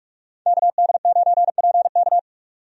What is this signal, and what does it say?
Morse code, heard as a keyed tone, about 46 wpm, 700 Hz. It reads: KD9PK
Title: USA Call Signs - 5 Characters - 45wpm